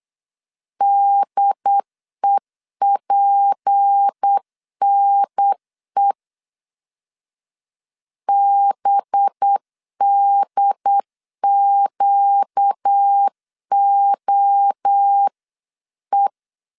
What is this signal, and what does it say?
Morse code, heard as a keyed tone, about 8 wpm, 800 Hz. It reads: DEPNE BDQOE
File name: CW / DET / 1 6 1500C S B 8wpm 5142